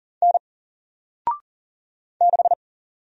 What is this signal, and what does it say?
Morse code, heard as a keyed tone, about 40 wpm, 700 Hz. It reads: N 6
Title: Single Letter-Number 40wpm